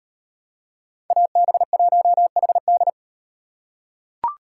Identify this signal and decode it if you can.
Morse code, heard as a keyed tone, about 38 wpm, 700 Hz. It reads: AB1HD